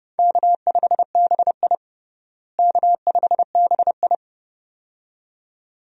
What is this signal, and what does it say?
Morse code, heard as a keyed tone, about 30 wpm, 700 Hz. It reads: K5BI K5BI
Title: USA Call Signs - 4 Characters - Repeated 2x - 30wpm